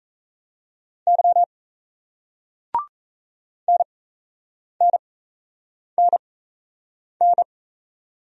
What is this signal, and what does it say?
Morse code, heard as a keyed tone, about 39 wpm, spaced as for 8 wpm, 700 Hz. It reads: Y NNNN